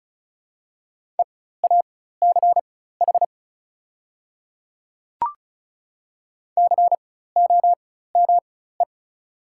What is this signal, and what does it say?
Morse code, heard as a keyed tone, about 36 wpm, spaced as for 16 wpm, 700 Hz. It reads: EACH COME
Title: 4x Character Spacing - Top 100 Words - 35wpm